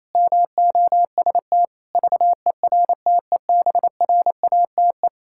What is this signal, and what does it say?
Morse code, heard as a keyed tone, about 28 wpm, 700 Hz. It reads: MOST VERTEBRATE
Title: Pangram Challenge v2 - 28wpm